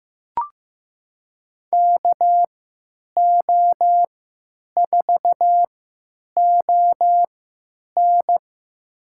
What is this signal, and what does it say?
Morse code, heard as a keyed tone, about 15 wpm, spaced as for 9 wpm, 700 Hz. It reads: KO4ON
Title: USA Call Signs - 3x Character Spacing - 15wpm